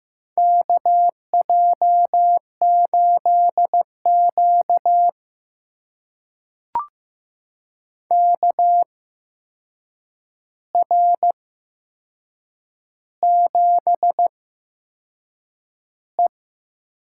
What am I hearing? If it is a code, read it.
Morse code, heard as a keyed tone, about 15 wpm, spaced as for 6 wpm, 700 Hz. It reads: KJ8Q KR7E
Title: USA Call Signs - 4 Characters - 8x Character Spacing - 15wpm